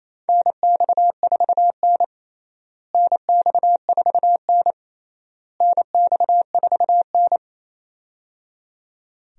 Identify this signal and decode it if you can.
Morse code, heard as a keyed tone, about 28 wpm, 700 Hz. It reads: NX4N NX4N NX4N
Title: USA Call Signs - 4 Characters - Repeated 3x - 28wpm